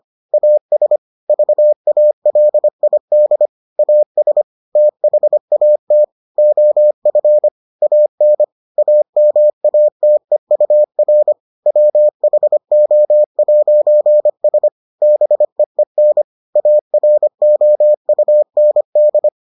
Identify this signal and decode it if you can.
Morse code, heard as a keyed tone, about 25 wpm, 600 Hz. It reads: AS VALID AS THAT OF AN AMATEUR WHO'S BEEN AROUND